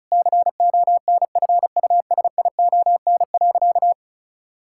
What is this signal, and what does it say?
Morse code, heard as a keyed tone, about 34 wpm, 700 Hz. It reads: CONFUSION.
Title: Sentences from Top 1600 Words 35wpm